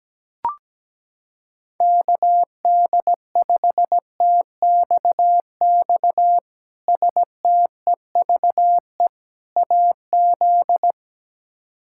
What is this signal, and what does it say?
Morse code, heard as a keyed tone, about 17 wpm, 700 Hz. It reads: KD5TXX STEVE AZ
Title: K1USN SST QSO Practice - 17wpm